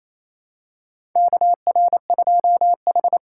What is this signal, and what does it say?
Morse code, heard as a keyed tone, about 28 wpm, 700 Hz. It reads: KR2H
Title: USA Call Signs - 4 Characters - 3x Character Spacing - 28wpm